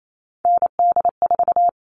Morse code, heard as a keyed tone, about 28 wpm, 700 Hz. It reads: ND4
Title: All Cut Numbers - 3-Digit 28wpm